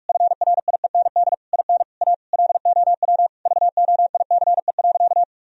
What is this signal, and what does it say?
Morse code, heard as a keyed tone, about 46 wpm, 700 Hz. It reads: FRIEND IN A LOW VOICE.
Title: Sentences from Top 1000 Words - Review Part 1 - 45wpm